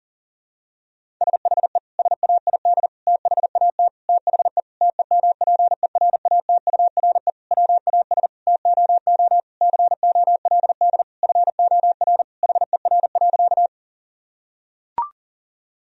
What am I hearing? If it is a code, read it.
Morse code, heard as a keyed tone, about 40 wpm, 700 Hz. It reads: SHE SAID THAT THE TEMPERATURE WAS TOO COLD FOR HER.
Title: Sentences from Top 1000 Words - Review Part 4 - 40wpm